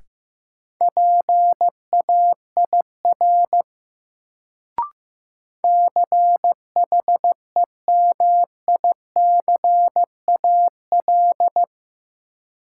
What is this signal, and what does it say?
Morse code, heard as a keyed tone, about 15 wpm, 700 Hz. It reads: PAIR CHEMICAL
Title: Sentences from Top 1200 Words - New Word Drill - 15wpm